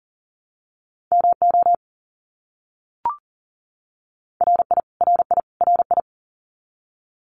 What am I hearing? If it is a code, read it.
Morse code, heard as a keyed tone, about 40 wpm, 700 Hz. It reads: MO RI RI RI
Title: US States Repeated 3x 40wpm